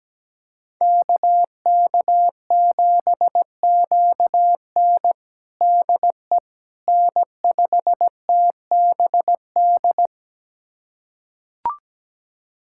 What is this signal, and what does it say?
Morse code, heard as a keyed tone, about 17 wpm, 700 Hz. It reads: KK7QN DE N5TBD
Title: USA Callsign DE Callsign - 5 Characters - 17wpm